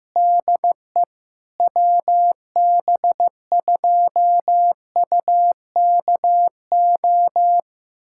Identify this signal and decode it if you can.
Morse code, heard as a keyed tone, about 15 wpm, 700 Hz. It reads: DE WB2UKO
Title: USA Callsign DE Callsign - 6 Characters - 15wpm